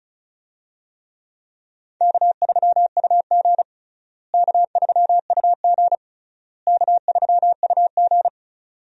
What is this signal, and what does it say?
Morse code, heard as a keyed tone, about 36 wpm, 700 Hz. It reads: K3UG K3UG K3UG
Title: USA Call Signs - 4 Characters - Repeated 3x - 35wpm